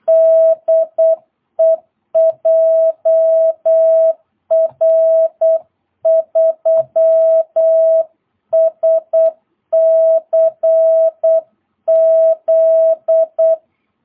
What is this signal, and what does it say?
Morse code, heard as a keyed tone, about 8 wpm, 660 Hz. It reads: DE JR3SCZ
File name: jh8cbh / 2022cw